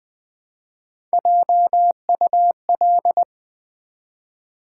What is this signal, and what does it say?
Morse code, heard as a keyed tone, about 20 wpm, 690 Hz. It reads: JUL
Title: Build A Word - 1 Letter At A Time - 4-Letter Words - 20wpm